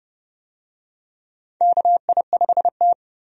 Morse code, heard as a keyed tone, about 29 wpm, 700 Hz. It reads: KI5T
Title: USA Call Signs - 4 Characters - 30wpm